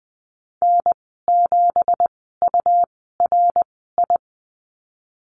Morse code, heard as a keyed tone, about 19 wpm, spaced as for 14 wpm, 700 Hz. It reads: N7URI
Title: 186 Lesson 30g - Letter b - Call Signs 20wpm